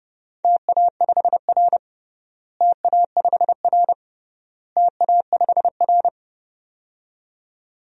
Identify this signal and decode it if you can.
Morse code, heard as a keyed tone, about 30 wpm, 700 Hz. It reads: TA5R TA5R TA5R
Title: DXCC-100 Call Signs - 4 Characters - Repeated 3x - 30wpm